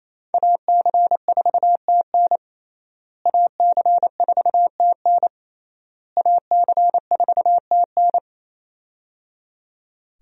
Morse code, heard as a keyed tone, about 28 wpm, 700 Hz. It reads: AC4TN AC4TN AC4TN
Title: USA Call Signs - 5 Characters - Repeated 3x - 28wpm